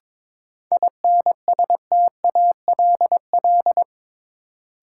Morse code, heard as a keyed tone, about 21 wpm, 700 Hz. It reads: INSTALL
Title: Sentences from Top 1600 Words - New Word Drill Alt - 22wpm